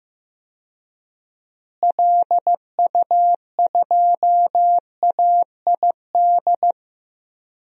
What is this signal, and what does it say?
Morse code, heard as a keyed tone, about 15 wpm, 700 Hz. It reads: LU2AID
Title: DXCC-100 Call Signs - 6 Characters - Repeated 3x - 15wpm